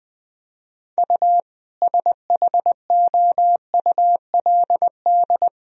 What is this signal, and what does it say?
Morse code, heard as a keyed tone, about 19 wpm, 700 Hz. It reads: U SHOULD
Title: Sentences from Top 400 Words and Abbreviations 20wpm